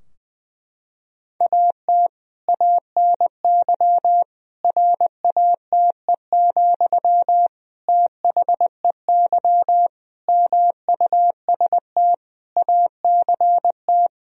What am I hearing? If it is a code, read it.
Morse code, heard as a keyed tone, about 20 wpm, 700 Hz. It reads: AT ANY RATE, THEY MUST ACT
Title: Sentences from Top 1000 Words - Review Part 4 - 20wpm